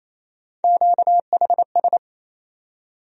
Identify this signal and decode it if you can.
Morse code, heard as a keyed tone, about 28 wpm, 700 Hz. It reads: QHS
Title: USA Call Sign Postfixes - 3 Letters 28wpm